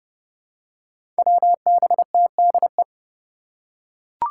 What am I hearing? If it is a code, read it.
Morse code, heard as a keyed tone, about 30 wpm, 700 Hz. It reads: WBTDE
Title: Five Challenging Letter-Number Combinations 30wpm